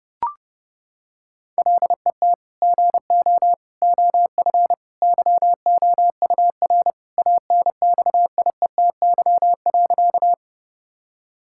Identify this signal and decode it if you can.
Morse code, heard as a keyed tone, about 32 wpm, 700 Hz. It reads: LET GO OF YOUR ANXIETY.
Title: Sentences from Top 1600 Words 30wpm